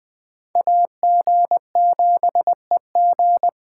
Morse code, heard as a keyed tone, about 20 wpm, 700 Hz. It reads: AG7EG